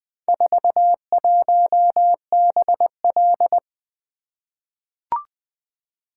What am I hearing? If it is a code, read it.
Morse code, heard as a keyed tone, about 20 wpm, 700 Hz. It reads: 41BL
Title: Four Letter-Number Combinations - 20wpm